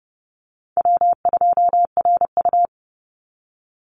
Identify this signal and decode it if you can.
Morse code, heard as a keyed tone, about 28 wpm, 700 Hz. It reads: W2RU
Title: Easy Call Signs Repeated 2x - 30wpm